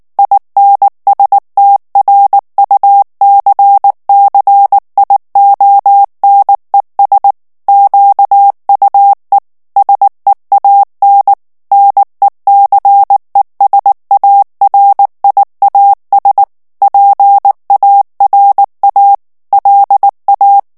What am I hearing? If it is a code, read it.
Morse code, heard as a keyed tone, about 19 wpm, 800 Hz. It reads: INSTRUCCIONES QUE SEAN NECESARIAS PARA LA
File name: morse mp3 / ESPAGNOL TEXTE EN CLAIR 2 VITESSE 18 MOTS MINUTES